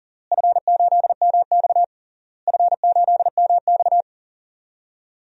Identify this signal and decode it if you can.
Morse code, heard as a keyed tone, about 40 wpm, 700 Hz. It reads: F8MX F8MX
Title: DXCC-100 Call Signs - 4 Characters - Repeated 3x - 40wpm